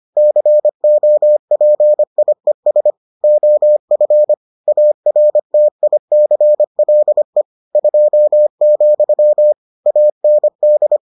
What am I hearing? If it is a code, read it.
Morse code, heard as a keyed tone, about 25 wpm, 600 Hz. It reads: COPIES OF ARTICLE 2, AND